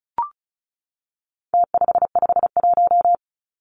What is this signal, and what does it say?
Morse code, heard as a keyed tone, about 34 wpm, 700 Hz. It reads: T551
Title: Numbers - Contest Style - 4-Digit 35wpm